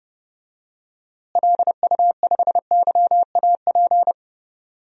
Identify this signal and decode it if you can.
Morse code, heard as a keyed tone, about 30 wpm, 700 Hz. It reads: LU5YAP
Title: DXCC-100 Call Signs - 6 Characters - Repeated 3x - 30wpm